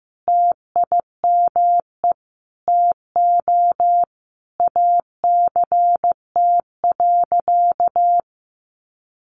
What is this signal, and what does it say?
Morse code, heard as a keyed tone, about 15 wpm, 700 Hz. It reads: TIME TO ACT.
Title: Sentences from Top 200 Words 15wpm